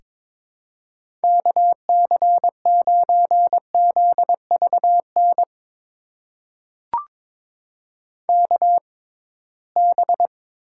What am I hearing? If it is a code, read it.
Morse code, heard as a keyed tone, about 22 wpm, spaced as for 17 wpm, 700 Hz. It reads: KC9ZVN KB
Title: USA Call Signs - 6 Characters - 6x Character Spacing - 22wpm